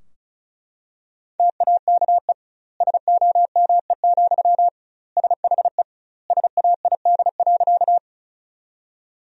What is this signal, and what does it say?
Morse code, heard as a keyed tone, about 36 wpm, 700 Hz. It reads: TAKE SOME, SHE SAID.
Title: 2x Word-Spacing - Sentences from Top 200 Words - 35wpm